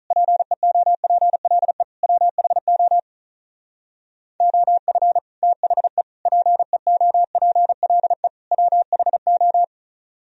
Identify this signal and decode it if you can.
Morse code, heard as a keyed tone, about 42 wpm, 700 Hz. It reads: PEOPLE WHO OF THE PEOPLE WHO
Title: Speed-Racing - Sets of 4 Words 35wpm